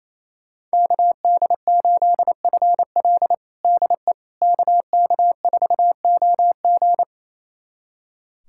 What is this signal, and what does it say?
Morse code, heard as a keyed tone, about 27 wpm, 700 Hz. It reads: KD8FL DE KK4OG
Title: USA Callsign DE Callsign - 5 Characters - 28wpm